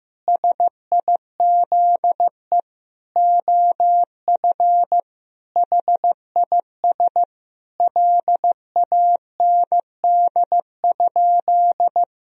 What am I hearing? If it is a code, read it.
Morse code, heard as a keyed tone, about 16 wpm, 700 Hz. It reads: SIZE OF HIS LAND?
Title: Sentences from Top 1000 Words - Review Part 2 - 15wpm